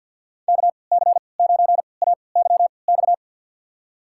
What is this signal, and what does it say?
Morse code, heard as a keyed tone, about 47 wpm, spaced as for 28 wpm, 700 Hz. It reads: KC9AYX
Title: USA Call Signs - 6 Characters - 3x Character Spacing - 50wpm